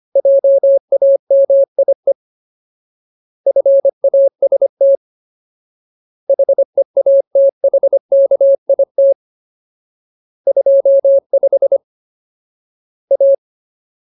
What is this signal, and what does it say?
Morse code, heard as a keyed tone, about 25 wpm, 550 Hz. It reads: JAMIE FAST HEATHKIT 25 A
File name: Ultimate Word List / 25WPM Ultimate